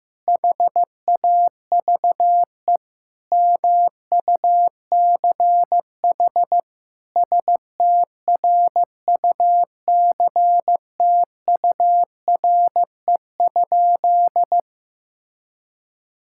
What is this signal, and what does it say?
Morse code, heard as a keyed tone, about 15 wpm, 700 Hz. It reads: HAVE MUCH STRUCTURE?
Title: Sentences from Top 1000 Words - Review Part 6 - 15wpm